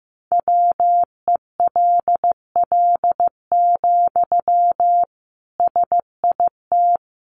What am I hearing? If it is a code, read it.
Morse code, heard as a keyed tone, about 15 wpm, 700 Hz. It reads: WELL, SIT
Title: Sentences from Top 500 Words 15wpm